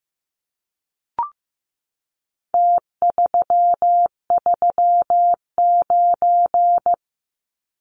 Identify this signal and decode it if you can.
Morse code, heard as a keyed tone, about 15 wpm, 700 Hz. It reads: T339
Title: Numbers - Contest Style - 4-Digit 15wpm